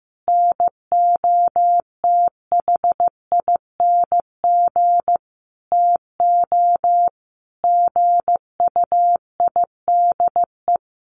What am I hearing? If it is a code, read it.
Morse code, heard as a keyed tone, about 15 wpm, 700 Hz. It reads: NOTHING TO GUIDE
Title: Sentences from Top 800 Words 15wpm